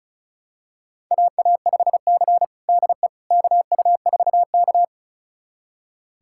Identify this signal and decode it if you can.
Morse code, heard as a keyed tone, about 36 wpm, 700 Hz. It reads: AA5C DE KU4K
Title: USA Callsign DE Callsign - 4 Characters - 35wpm